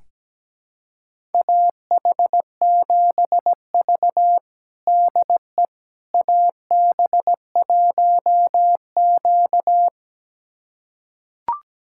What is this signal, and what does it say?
Morse code, heard as a keyed tone, about 17 wpm, 700 Hz. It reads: AH7V DE AB1Q
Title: USA Callsign DE Callsign - 4 Characters - 17wpm